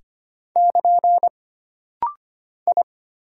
Morse code, heard as a keyed tone, about 25 wpm, 700 Hz. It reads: <KN> I
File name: Single Character (Letter:Number:Punctuation:Prosign) - Rapid-Fire - 25wpm